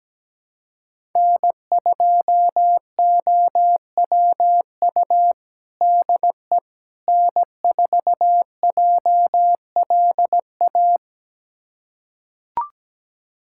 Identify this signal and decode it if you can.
Morse code, heard as a keyed tone, about 17 wpm, 700 Hz. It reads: N2OWU DE N4JLA
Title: USA Callsign DE Callsign - 5 Characters - 17wpm